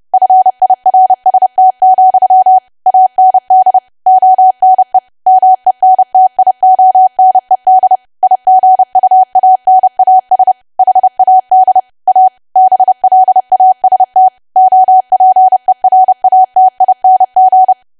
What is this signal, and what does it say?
Morse code, heard as a keyed tone, about 30 wpm, 750 Hz. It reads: FIRST, AND ONE MENTIONED IGUANAS HAD A BLAST OPERATING
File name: cwa / W1AW-Level3 / 30WPM